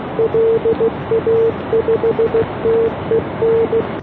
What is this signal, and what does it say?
Morse code, heard as a keyed tone, about 15 wpm, 470 Hz. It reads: LA5TEN